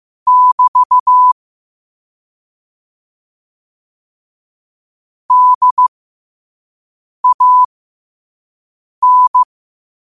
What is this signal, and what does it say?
Morse code, heard as a keyed tone, about 15 wpm, spaced as for 5 wpm, 1000 Hz. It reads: <BT> DAN
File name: qotdfiles / 5wpm